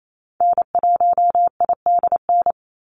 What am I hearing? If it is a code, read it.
Morse code, heard as a keyed tone, about 28 wpm, 700 Hz. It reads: N1IDN
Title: NA SOTA Activator Callsigns 28wpm